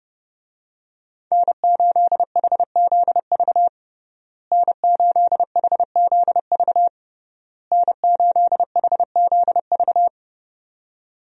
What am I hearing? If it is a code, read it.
Morse code, heard as a keyed tone, about 30 wpm, 700 Hz. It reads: N8HZV N8HZV N8HZV
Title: USA Call Signs - 5 Characters - Repeated 3x - 30wpm